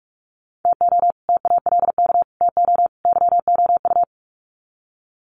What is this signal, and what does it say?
Morse code, heard as a keyed tone, about 44 wpm, 700 Hz. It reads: TO TALK TO YOU
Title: Sets of 4 Words 45wpm